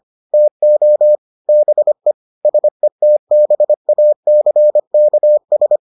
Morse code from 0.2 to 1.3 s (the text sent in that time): TO